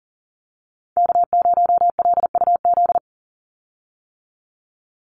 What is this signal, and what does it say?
Morse code, heard as a keyed tone, about 40 wpm, 700 Hz. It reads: K0LUZ
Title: Call Signs - 40wpm